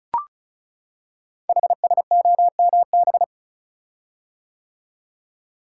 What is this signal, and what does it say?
Morse code, heard as a keyed tone, about 36 wpm, 700 Hz. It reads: HSOMB
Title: Code Groups - 5 Letters - 35wpm